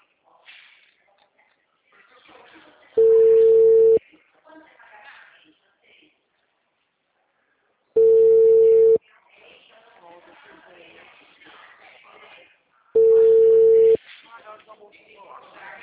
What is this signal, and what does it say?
Ringback tone (China, 450 Hz), with speech.